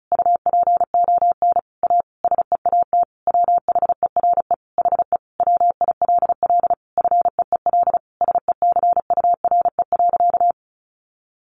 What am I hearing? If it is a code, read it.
Morse code, heard as a keyed tone, about 34 wpm, 700 Hz. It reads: UPON A SEAT WHERE HE WILL FEEL SECURE.
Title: Sentences from Top 1500 Words 35wpm